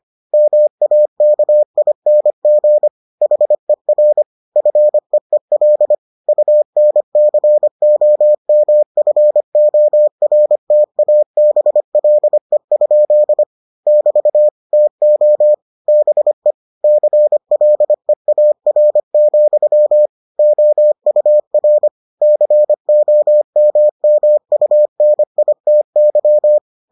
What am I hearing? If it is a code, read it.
Morse code, heard as a keyed tone, about 25 wpm, 600 Hz. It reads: MAKING HER FEEL UNCOMFORTABLE? <BT> TO BE CLEAR, OUR COMMUNITY